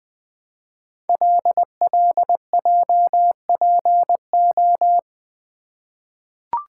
Morse code, heard as a keyed tone, about 20 wpm, 700 Hz. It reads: LLJPO
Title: Code Groups - 5 Letters - 20wpm